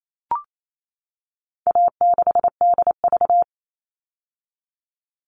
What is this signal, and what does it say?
Morse code, heard as a keyed tone, about 28 wpm, 700 Hz. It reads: A6DV
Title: All Cut Numbers - 4-Digit 28wpm